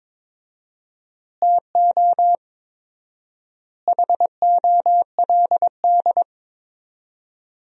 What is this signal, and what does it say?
Morse code, heard as a keyed tone, about 22 wpm, 700 Hz. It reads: TO HOLD
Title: 4x Word-Spacing - Sentences from Top 400 Words - 22wpm